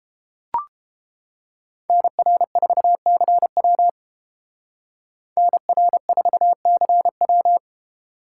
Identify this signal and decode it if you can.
Morse code, heard as a keyed tone, about 28 wpm, 700 Hz. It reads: NR4CW NR4CW